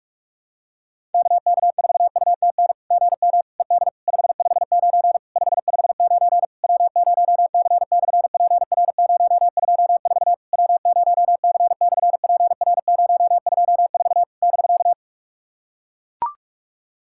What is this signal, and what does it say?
Morse code, heard as a keyed tone, about 44 wpm, 700 Hz. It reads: KK4UTN GM ED 559 559 W0C/PR024 W0C/PR024 <BK>
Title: SOTA QSO Practice 45wpm